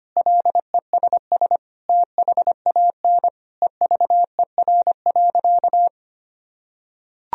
Morse code, heard as a keyed tone, about 25 wpm, 700 Hz. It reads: LESS THAN EVER.
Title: Sentences from Top 1000 Words - Review Part 2 - 25wpm